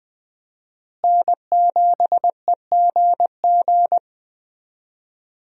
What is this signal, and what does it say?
Morse code, heard as a keyed tone, about 20 wpm, 700 Hz. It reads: N7EGG
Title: USA Call Signs - 5 Characters - Repeated 3x - 20wpm